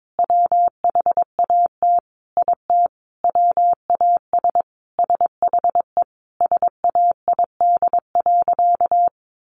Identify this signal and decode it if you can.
Morse code, heard as a keyed tone, about 22 wpm, 700 Hz. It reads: WHAT IT WAS SHE SAID.
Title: Sentences from Top 400 Words 22wpm